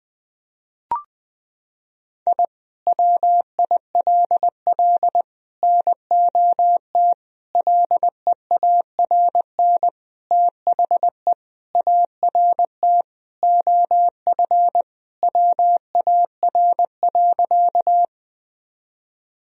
Morse code, heard as a keyed tone, about 20 wpm, 700 Hz. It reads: I WILL NOT LEARN THE ART OF WAR.
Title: Sentences from Top 1000 Words - Review Part 3 - 20wpm